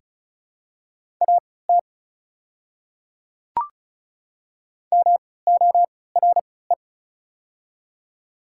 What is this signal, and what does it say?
Morse code, heard as a keyed tone, about 36 wpm, spaced as for 20 wpm, 700 Hz. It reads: AT MORE
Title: 3x Character Spacing - Top 100 Words - 35wpm